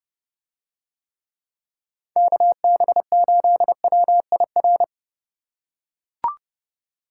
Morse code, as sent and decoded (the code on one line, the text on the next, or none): -.- -... ---.. .-- .. .-.
KB8WIR